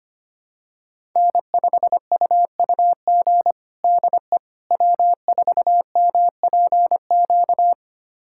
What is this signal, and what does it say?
Morse code, heard as a keyed tone, about 25 wpm, 700 Hz. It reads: N5UUG DE W4MPQ
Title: USA Callsign DE Callsign - 5 Characters - 25wpm